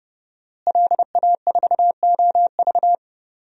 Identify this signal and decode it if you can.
Morse code, heard as a keyed tone, about 30 wpm, 700 Hz. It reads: LA4OV
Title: DXCC-100 Call Signs - 5 Characters - Repeated 3x - 30wpm